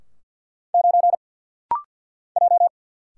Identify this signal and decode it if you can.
Morse code, heard as a keyed tone, about 48 wpm, 700 Hz. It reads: 9 J
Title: Single Letter-Number - Rapid-Fire 50wpm